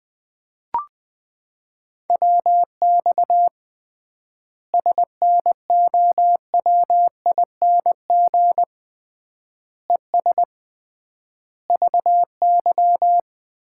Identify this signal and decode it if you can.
Morse code, heard as a keyed tone, about 20 wpm, 700 Hz. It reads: WX SNOWING ES VY
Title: Common QSO Phrases - 3x Word-Spacing - 20wpm